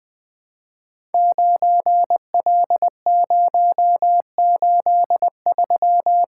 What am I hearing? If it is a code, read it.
Morse code, heard as a keyed tone, about 20 wpm, 700 Hz. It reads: 9L083
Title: Code Groups - 5 Letter:Numbers - 20wpm